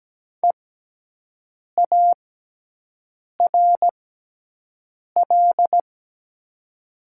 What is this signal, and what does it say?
Morse code, heard as a keyed tone, about 17 wpm, spaced as for 6 wpm, 700 Hz. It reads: EARL
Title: 6x Character Spacing - Top 500 Words - 17wpm